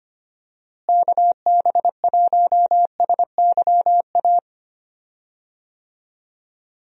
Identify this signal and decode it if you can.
Morse code, heard as a keyed tone, about 25 wpm, 700 Hz. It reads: KB1SYA
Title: USA Call Signs - 6 Characters - Repeated 2x - 25wpm